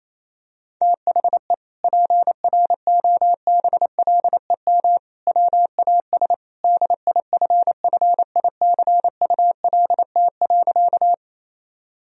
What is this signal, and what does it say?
Morse code, heard as a keyed tone, about 28 wpm, 700 Hz. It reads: THE PROBLEM WAS DIFFICULT.
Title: Sentences from Top 1000 Words - Review Part 2 - 28wpm